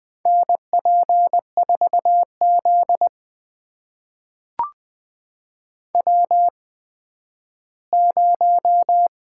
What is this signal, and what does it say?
Morse code, heard as a keyed tone, about 20 wpm, spaced as for 12 wpm, 700 Hz. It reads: NP4Z W0
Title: USA Call Signs - 8x Character Spacing - 20wpm